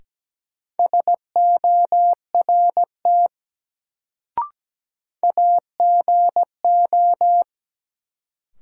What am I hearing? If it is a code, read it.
Morse code, heard as a keyed tone, about 17 wpm, 700 Hz. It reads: SORT AGO
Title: Sets of 2 Words - New Word Drill - 17wpm